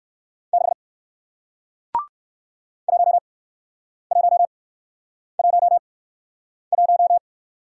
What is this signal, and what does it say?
Morse code, heard as a keyed tone, about 61 wpm, 700 Hz. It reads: 6 1 1 1 1